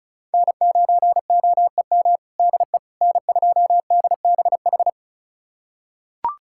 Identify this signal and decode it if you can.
Morse code, heard as a keyed tone, about 36 wpm, 700 Hz. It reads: N9OEM DE N2DBH
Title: USA Callsign DE Callsign - 5 Characters - 35wpm